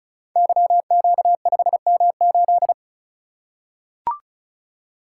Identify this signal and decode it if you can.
Morse code, heard as a keyed tone, about 36 wpm, 700 Hz. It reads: YQ5M8
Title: Code Groups - 5 Letter:Numbers - 35wpm